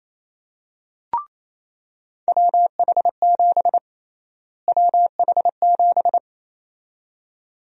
Morse code, heard as a keyed tone, about 28 wpm, 700 Hz. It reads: WH7 WH7
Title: USA Call Sign Prefixes - Repeated 2x - 28wpm